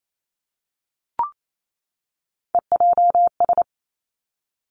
Morse code, heard as a keyed tone, about 28 wpm, 700 Hz. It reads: EJS
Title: Three Challenging Letter-Number Combinations 28wpm